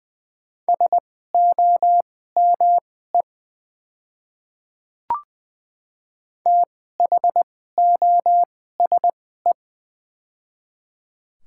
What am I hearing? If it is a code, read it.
Morse code, heard as a keyed tone, about 20 wpm, spaced as for 14 wpm, 700 Hz. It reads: SOME THOSE
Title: 2x Character Spacing - Top 300 Words - 20wpm